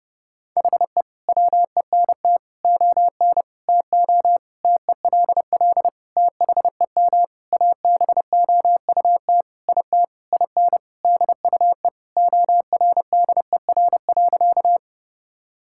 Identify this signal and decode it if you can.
Morse code, heard as a keyed tone, about 32 wpm, 700 Hz. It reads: HE WENT ON TO TELL THEM ABOUT IT IN DUE ORDER.
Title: Sentences from Top 1000 Words - Review Part 6 - 30wpm